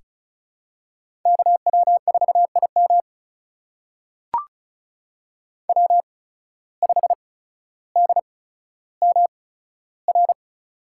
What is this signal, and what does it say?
Morse code, heard as a keyed tone, about 36 wpm, spaced as for 15 wpm, 690 Hz. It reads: KW4IM W5DMR
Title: USA Call Signs - 5 Characters - 8x Character Spacing - 35wpm